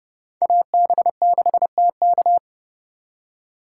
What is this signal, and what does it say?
Morse code, heard as a keyed tone, about 34 wpm, 700 Hz. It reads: AB6TK
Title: Speed Racing - USA 2x2 Call Signs 30wpm